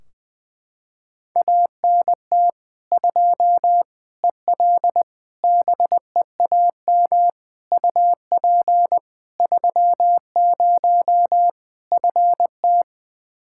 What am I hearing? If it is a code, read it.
Morse code, heard as a keyed tone, about 20 wpm, 700 Hz. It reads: ANT 2 EL BEAM UP 30 FT
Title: Practice QSO - Exchange 2 - Info - 20wpm